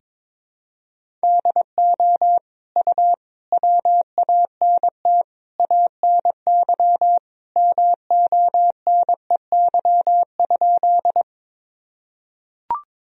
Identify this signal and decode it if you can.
Morse code, heard as a keyed tone, about 22 wpm, 700 Hz. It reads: DO U WANT ANY MONEY?
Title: Sentences from Top 400 Words and Abbreviations 22wpm